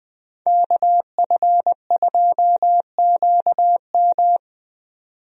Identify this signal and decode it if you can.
Morse code, heard as a keyed tone, about 20 wpm, 700 Hz. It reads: KF2QM
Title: USA Call Signs - 5 Characters - 20wpm